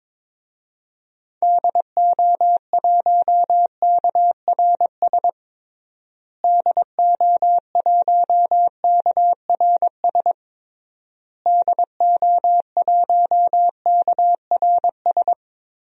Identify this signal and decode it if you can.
Morse code, heard as a keyed tone, about 22 wpm, 700 Hz. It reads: DO1KRS DO1KRS DO1KRS
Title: DXCC-100 Call Signs - 6 Characters - Repeated 3x - 22wpm